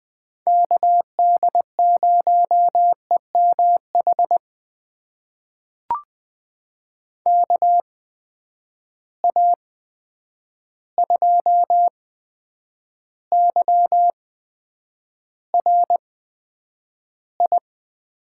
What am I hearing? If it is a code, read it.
Morse code, heard as a keyed tone, about 20 wpm, spaced as for 9 wpm, 700 Hz. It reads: KD0EMH KA2YRI
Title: USA Call Signs - 6 Characters - 8x Character Spacing - 20wpm